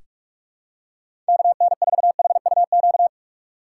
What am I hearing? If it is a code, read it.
Morse code, heard as a keyed tone, about 46 wpm, 700 Hz. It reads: KN4HUQ
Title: USA Call Signs - 6 Characters - 4x Character Spacing - 45wpm